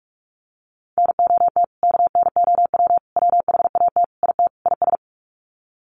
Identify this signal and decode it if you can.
Morse code, heard as a keyed tone, about 44 wpm, 700 Hz. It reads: NOT KNOW WHAT IT IS
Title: Sets of 5 Words Encore 45wpm